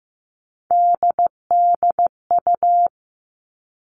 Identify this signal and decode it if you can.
Morse code, heard as a keyed tone, about 15 wpm, 700 Hz. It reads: DDU
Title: All Cut Numbers - 3-Digit 15wpm